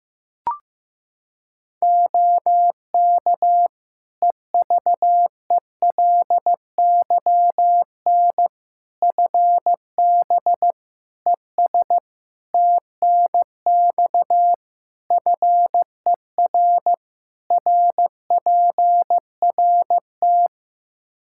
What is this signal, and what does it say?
Morse code, heard as a keyed tone, about 15 wpm, 700 Hz. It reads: OK EVELYN FB ES TNX FER RPRT
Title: Practice QSO - Exchange 2 - Info - 15wpm